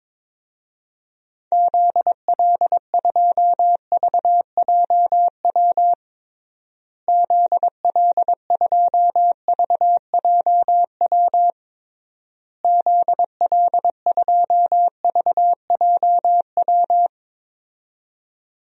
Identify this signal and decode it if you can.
Morse code, heard as a keyed tone, about 22 wpm, 700 Hz. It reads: ZL2VJW ZL2VJW ZL2VJW
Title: DXCC-100 Call Signs - 6 Characters - Repeated 3x - 22wpm